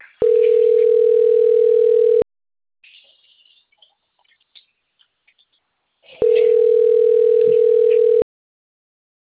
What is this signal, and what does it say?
Ringback tone (North America), 440 + 480 Hz.
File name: out- - - - - .85865